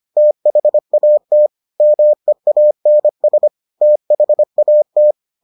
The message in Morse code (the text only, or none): THAT MEANS THAT